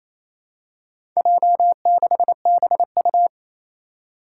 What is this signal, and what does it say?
Morse code, heard as a keyed tone, about 28 wpm, 700 Hz. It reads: J6BU
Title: Four Challenging Letter-Number Combinations 28wpm